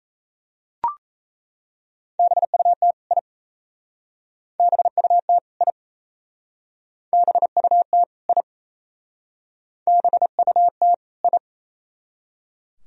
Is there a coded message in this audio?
Yes: Morse code from 2.2 to 11.4 s, heard as a keyed tone, about 35 wpm, 690 Hz. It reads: BUT I BUT I BUT I BUT I